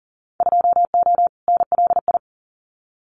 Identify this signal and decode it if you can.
Morse code, heard as a keyed tone, about 40 wpm, 700 Hz. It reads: 2O NLI
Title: Winter Field Day Exchanges 40wpm